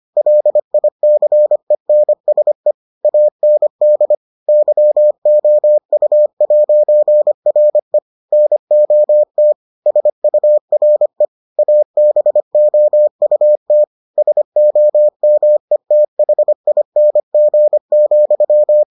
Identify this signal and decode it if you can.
Morse code, heard as a keyed tone, about 25 wpm, 600 Hz. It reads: LICENSE AND YOU'RE NOT SURE ABOUT SOMETHING,